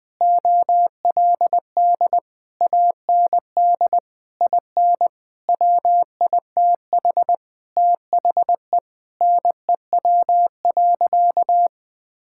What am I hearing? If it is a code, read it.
Morse code, heard as a keyed tone, about 20 wpm, 700 Hz. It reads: OLD AND IN WITH THE NEW.
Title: Sentences with 9-Word Phrases - Spoken Beginning - 20wpm